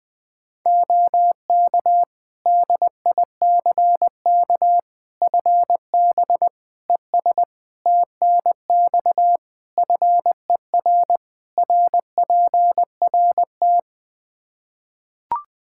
Morse code, heard as a keyed tone, about 21 wpm, 700 Hz. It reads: OK DICK FB ES TNX FER RPRT
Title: Practice QSO - Exchange 2 - Info - 20wpm